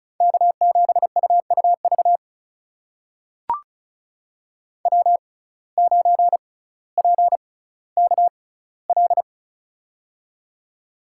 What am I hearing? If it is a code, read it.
Morse code, heard as a keyed tone, about 36 wpm, spaced as for 18 wpm, 700 Hz. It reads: K7UUV W9PKL
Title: USA Call Signs - 6x Character Spacing - 35wpm